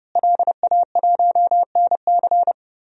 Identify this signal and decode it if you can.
Morse code, heard as a keyed tone, about 31 wpm, 700 Hz. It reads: LA1NC
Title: DXCC-100 Call Signs - 5 Characters - Repeated 2x - 30wpm